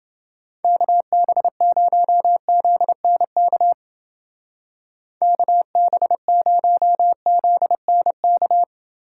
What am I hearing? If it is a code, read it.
Morse code, heard as a keyed tone, about 30 wpm, 700 Hz. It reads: KB0ZNK KB0ZNK